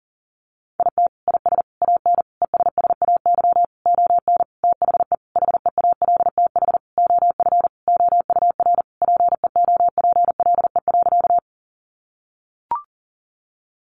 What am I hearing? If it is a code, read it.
Morse code, heard as a keyed tone, about 40 wpm, 700 Hz. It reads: IT IS AN ESSAY ON THE HEALTH OF OUR PEOPLE.
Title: Sentences from Top 1500 Words 40wpm